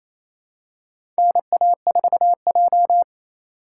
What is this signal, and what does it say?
Morse code, heard as a keyed tone, about 28 wpm, 700 Hz. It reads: NA4J
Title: USA Call Signs - 4 Characters - 4x Character Spacing - 28wpm